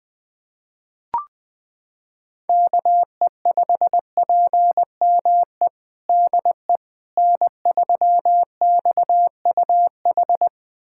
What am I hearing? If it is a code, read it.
Morse code, heard as a keyed tone, about 20 wpm, 700 Hz. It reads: KE5PME DE N3XUH
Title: USA Callsign DE Callsign 20wpm